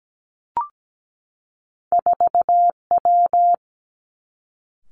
Morse code, heard as a keyed tone, about 17 wpm, 700 Hz. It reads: 4 W
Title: Two Challenging Letter-Number Combinations 17wpm